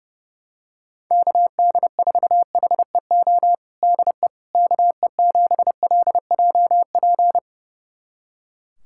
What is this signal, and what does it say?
Morse code, heard as a keyed tone, about 32 wpm, 700 Hz. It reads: KD4HEO DE KE7LJP
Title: USA Callsign DE Callsign 30wpm